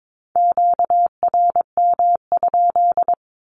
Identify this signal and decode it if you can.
Morse code, heard as a keyed tone, about 22 wpm, 700 Hz. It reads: QRM?
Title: Q-Codes 22wpm